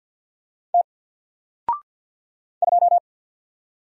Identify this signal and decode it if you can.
Morse code, heard as a keyed tone, about 50 wpm, 700 Hz. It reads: T 2
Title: Single Letter-Number 50wpm